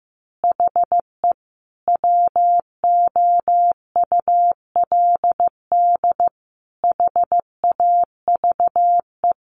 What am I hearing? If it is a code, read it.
Morse code, heard as a keyed tone, about 15 wpm, 700 Hz. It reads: HE WOULD HAVE
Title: A Sound Of Thunder 15wpm